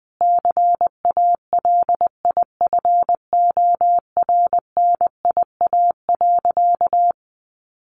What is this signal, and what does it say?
Morse code, heard as a keyed tone, about 20 wpm, 700 Hz. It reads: CALIFORNIA.
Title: Easy Phrases 20wpm